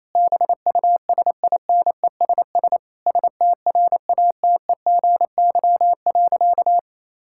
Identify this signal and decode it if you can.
Morse code, heard as a keyed tone, about 28 wpm, 700 Hz. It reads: BUSINESS STRATEGY.
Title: Sentences from Top 1000 Words - Review Part 6 - 28wpm